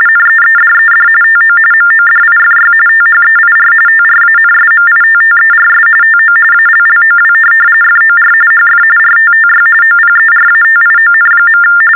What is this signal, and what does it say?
TTY/TDD text telephone tones, about 45.45 baud. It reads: NEEDED TO MODIFY THEIR TRANSMITTERS TO ALLOW FOR HF RTTY OPERATION.
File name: teletype-RTTY-45